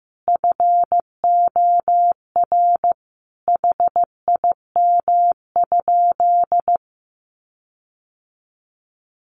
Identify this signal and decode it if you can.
Morse code, heard as a keyed tone, about 15 wpm, 700 Hz. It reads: FOR HIM?
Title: Sentences from Top 1300 Words 15wpm